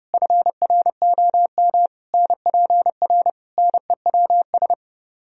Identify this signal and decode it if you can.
Morse code, heard as a keyed tone, about 33 wpm, 700 Hz. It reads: FROM NPR NEWS